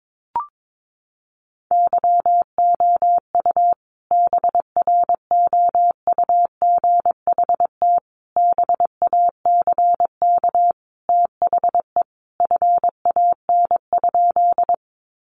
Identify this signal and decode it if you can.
Morse code, heard as a keyed tone, about 22 wpm, 700 Hz. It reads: YOU BROUGHT BACK THE FAN?
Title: Sentences from Top 1300 Words 22wpm